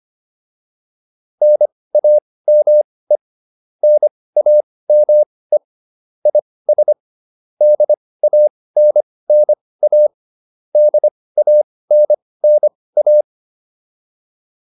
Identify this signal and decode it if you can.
Morse code, heard as a keyed tone, about 25 wpm, spaced as for 18 wpm, 600 Hz. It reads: NAME NAME IS DANNA DANNA